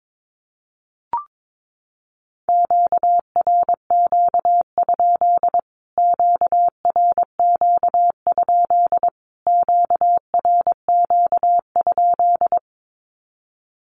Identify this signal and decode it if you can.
Morse code, heard as a keyed tone, about 22 wpm, 700 Hz. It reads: QRQ? QRQ? QRQ?
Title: Q-Codes Repeated 3x 22wpm